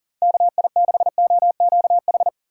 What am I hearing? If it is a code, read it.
Morse code, heard as a keyed tone, about 40 wpm, 700 Hz. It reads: KI6OQH
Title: USA Call Signs - 6 Characters - 3x Character Spacing - 40wpm